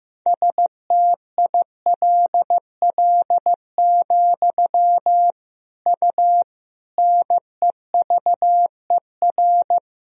Morse code, heard as a keyed tone, about 15 wpm, 700 Hz. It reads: STILL, U NEVER
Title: Sentences from Top 300 Words and Abbreviations 15wpm